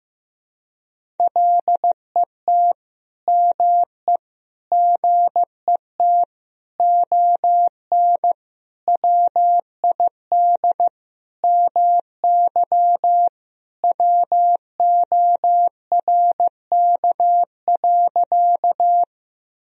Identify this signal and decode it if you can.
Morse code, heard as a keyed tone, about 16 wpm, 700 Hz. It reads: LET ME GET ON WID MY WORK.
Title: Sentences from Top 300 Words and Abbreviations 15wpm